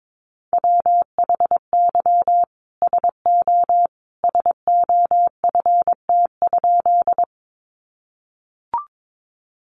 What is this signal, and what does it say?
Morse code, heard as a keyed tone, about 22 wpm, 700 Hz. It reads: WHY SO SOFT?
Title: Sentences from Top 1500 Words 22wpm